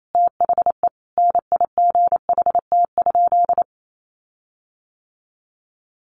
Morse code, heard as a keyed tone, about 28 wpm, 700 Hz. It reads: THE NIGHT?
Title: Sentences from Top 1300 Words 28wpm